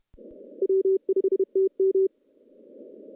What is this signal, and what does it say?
Morse code, heard as a keyed tone, about 30 wpm, 380 Hz. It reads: W5TM
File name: contests / CW-SS-k5tr- -1500